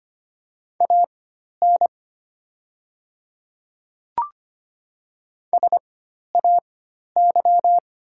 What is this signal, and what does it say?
Morse code, heard as a keyed tone, about 25 wpm, spaced as for 12 wpm, 700 Hz. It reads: AN SAY
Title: 4x Character Spacing - Top 200 Words - 25wpm